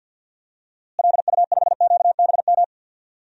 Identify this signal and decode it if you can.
Morse code, heard as a keyed tone, about 53 wpm, 700 Hz. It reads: LU5QBM
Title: DXCC-100 Call Signs - 6 Characters - Repeated 2x - 50wpm